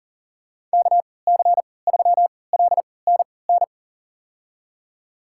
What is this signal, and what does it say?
Morse code, heard as a keyed tone, about 40 wpm, spaced as for 23 wpm, 700 Hz. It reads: KC3LNN
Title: USA Call Signs - 6 Characters - 3x Character Spacing - 40wpm